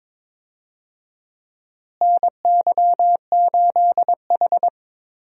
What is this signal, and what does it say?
Morse code, heard as a keyed tone, about 22 wpm, 700 Hz. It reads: NY8H
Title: USA Call Signs - 4 Characters - Repeated 3x - 22wpm